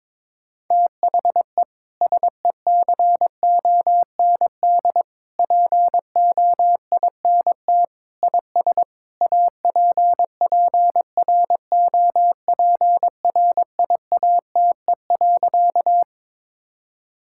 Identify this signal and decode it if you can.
Morse code, heard as a keyed tone, about 22 wpm, 700 Hz. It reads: THE SECOND POINT IS APPROPRIATE.
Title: Sentences from Top 1000 Words - Review Part 5 - 22wpm